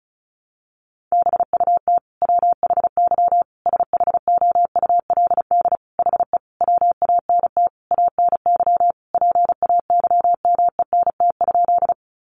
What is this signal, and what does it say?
Morse code, heard as a keyed tone, about 34 wpm, 700 Hz. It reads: BUT WHY SHOULD HE WANT ANY PAYMENT?
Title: Sentences from Top 1200 Words 35wpm